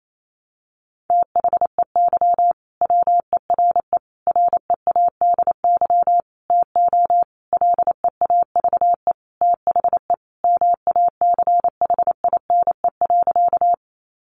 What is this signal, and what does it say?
Morse code, heard as a keyed tone, about 28 wpm, 700 Hz. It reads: THEY WERE READY TO LEAVE THE MACHINE.
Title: A Sound Of Thunder 28wpm